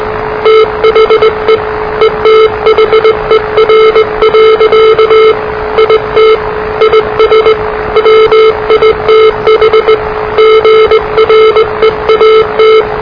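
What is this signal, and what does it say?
Morse code, heard as a keyed tone, about 20 wpm, 440 Hz, with noise. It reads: THE ETHER. IT IS WITH GREAT
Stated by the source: Morse challenge from NZ Net, the national amateur radio net of Aotearoa/New Zealand, which meets on 3535.0 kHz.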